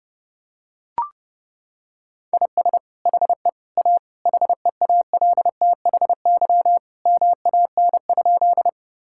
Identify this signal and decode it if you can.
Morse code, heard as a keyed tone, about 30 wpm, 700 Hz. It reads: IS HE A HEALTHY MAN?
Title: Sentences from Top 1000 Words - Review Part 5 - 30wpm